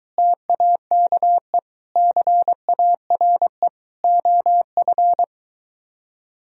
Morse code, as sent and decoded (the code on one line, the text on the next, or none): - .- -.- . / -.-. .- .-. / --- ..-.
TAKE CARE OF